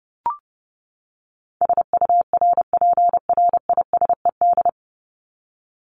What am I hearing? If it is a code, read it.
Morse code, heard as a keyed tone, about 30 wpm, 700 Hz. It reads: SURPRISED
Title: Top 1501-1750 Words 30wpm